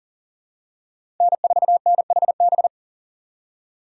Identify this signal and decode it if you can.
Morse code, heard as a keyed tone, about 40 wpm, 700 Hz. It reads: N4NHB
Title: USA Call Signs - 5 Characters - 4x Character Spacing - 40wpm